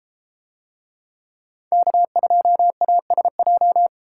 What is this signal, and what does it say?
Morse code, heard as a keyed tone, about 32 wpm, 700 Hz. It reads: K2ASJ